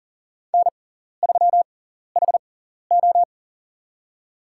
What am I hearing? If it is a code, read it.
Morse code, heard as a keyed tone, about 40 wpm, spaced as for 14 wpm, 700 Hz. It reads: N3HO
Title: USA Call Signs - 4 Characters - 6x Character Spacing - 40wpm